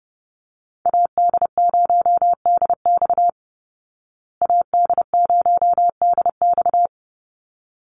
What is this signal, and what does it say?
Morse code, heard as a keyed tone, about 30 wpm, 700 Hz. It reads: AD0DX AD0DX
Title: Call Signs Repeated 3x - 30wpm